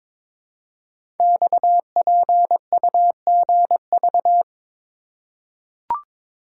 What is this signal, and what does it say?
Morse code, heard as a keyed tone, about 22 wpm, 700 Hz. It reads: XPUGV
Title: Code Groups - 5 Letters - 22wpm